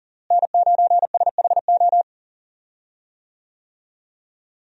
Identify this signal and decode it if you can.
Morse code, heard as a keyed tone, about 38 wpm, 700 Hz. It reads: N9SHO
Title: USA Call Signs - 5 Characters - Repeated 2x - 40wpm